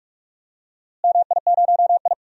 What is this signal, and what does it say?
Morse code, heard as a keyed tone, about 46 wpm, 700 Hz. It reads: MI0I
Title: DXCC-100 Call Signs - 4 Characters - Repeated 2x - 45wpm